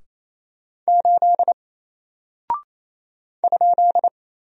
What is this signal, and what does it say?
Morse code, heard as a keyed tone, about 29 wpm, 700 Hz. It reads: 8 ?
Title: Single Character (Letter:Number:Punctuation:Prosign) Alt 28wpm